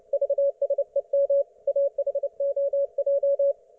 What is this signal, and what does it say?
Morse code, heard as a keyed tone, about 29 wpm, 560 Hz. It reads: VSEM AHOJ